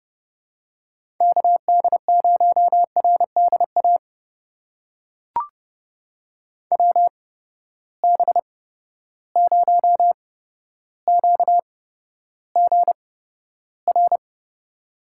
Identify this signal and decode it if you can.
Morse code, heard as a keyed tone, about 30 wpm, spaced as for 13 wpm, 700 Hz. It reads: KD0RDA WB0QGR
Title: USA Call Signs - 6 Characters - 8x Character Spacing - 30wpm